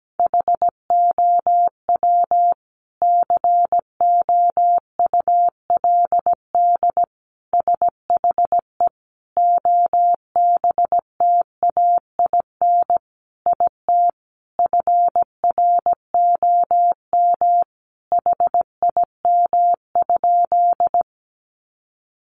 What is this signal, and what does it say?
Morse code, heard as a keyed tone, about 17 wpm, 700 Hz. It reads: HOW COULD SHE OBTAIN IT FROM HIM?
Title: Sentences from Top 1300 Words 17wpm